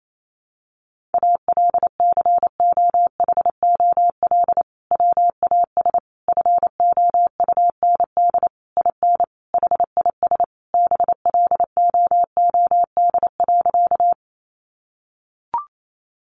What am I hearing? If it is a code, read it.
Morse code, heard as a keyed tone, about 28 wpm, 700 Hz. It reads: ALCOHOL WAS FOUND IN HIS BLOOD.
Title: Sentences from Top 1400 Words 28wpm